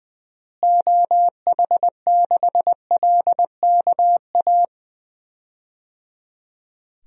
Morse code, heard as a keyed tone, about 20 wpm, 700 Hz. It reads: OH6LKA
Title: DXCC-100 Call Signs - 6 Characters - Repeated 3x - 20wpm